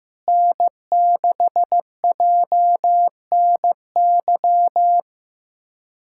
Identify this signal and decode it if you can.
Morse code, heard as a keyed tone, about 15 wpm, 700 Hz. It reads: N6JNY
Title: USA Callsign DE Callsign - 5 Characters - 15wpm